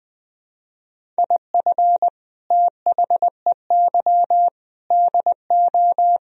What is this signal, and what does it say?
Morse code, heard as a keyed tone, about 19 wpm, 700 Hz. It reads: IF THEY DO